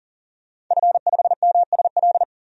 Morse code, heard as a keyed tone, about 40 wpm, 700 Hz. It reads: F5MSL
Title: DXCC-100 Call Signs - 5 Characters - 40wpm